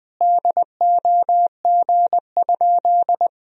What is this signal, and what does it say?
Morse code, heard as a keyed tone, about 20 wpm, 700 Hz. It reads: DOG?
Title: Sentences from Top 1000 Words - Review Part 2 - 20wpm